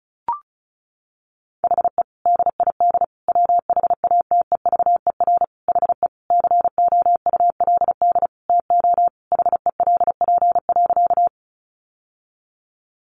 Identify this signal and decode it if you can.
Morse code, heard as a keyed tone, about 34 wpm, 700 Hz. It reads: HE DID WHATEVER HE COULD TO HELP.
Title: Sentences from Top 1100 Words 35wpm